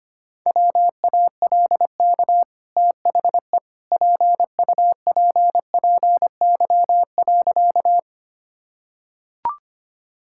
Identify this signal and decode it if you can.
Morse code, heard as a keyed tone, about 25 wpm, 690 Hz. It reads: WALK THE PUPPY.
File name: Easy Sentences with Hints - 25wpm